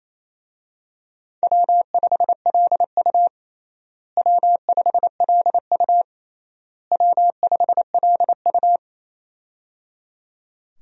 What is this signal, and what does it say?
Morse code, heard as a keyed tone, about 28 wpm, 700 Hz. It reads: W5LU W5LU W5LU
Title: USA Call Signs - 4 Characters - Repeated 3x - 28wpm